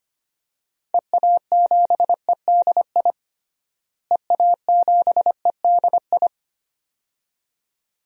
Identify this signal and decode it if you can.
Morse code, heard as a keyed tone, about 25 wpm, 700 Hz. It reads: EA7EDI EA7EDI
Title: DXCC-100 Call Signs - 6 Characters - Repeated 2x - 25wpm